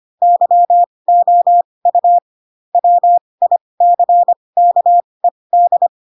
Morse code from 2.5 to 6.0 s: WICKED